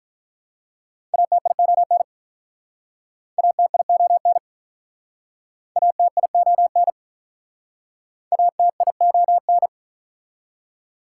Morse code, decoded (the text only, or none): ATION ATION ATION ATION